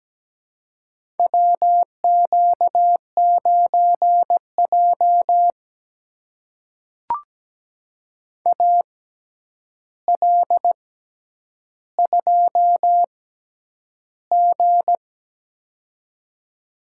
Morse code, heard as a keyed tone, about 17 wpm, spaced as for 9 wpm, 700 Hz. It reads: WQ9J AL2G
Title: USA Call Signs - 4 Characters - 6x Character Spacing - 17wpm